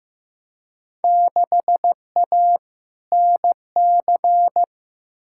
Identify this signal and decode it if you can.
Morse code, heard as a keyed tone, about 15 wpm, 700 Hz. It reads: 6A NC
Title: Field Day Exchanges Encore - 15wpm